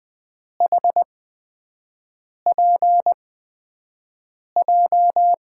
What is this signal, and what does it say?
Morse code, heard as a keyed tone, about 20 wpm, spaced as for 5 wpm, 700 Hz. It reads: HPJ
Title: USA Call Signs - 6 Characters - 8x Character Spacing - 20wpm